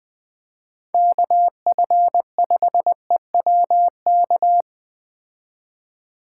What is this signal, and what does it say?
Morse code, heard as a keyed tone, about 20 wpm, 700 Hz. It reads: KF5EWK
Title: Speed Racing - USA 2x3 Call Signs 17wpm